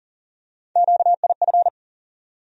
Morse code, heard as a keyed tone, about 40 wpm, 700 Hz. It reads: QIF
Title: USA Call Sign Postfixes - 3 Letters 40wpm